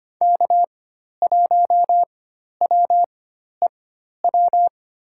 Morse code, heard as a keyed tone, about 25 wpm, spaced as for 12 wpm, 700 Hz. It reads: K1WEW
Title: USA Call Signs - 4x Character Spacing - 25wpm